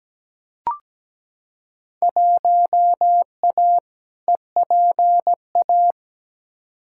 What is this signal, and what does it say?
Morse code, heard as a keyed tone, about 17 wpm, 700 Hz. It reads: 1A EPA
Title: Field Day Exchanges - 17wpm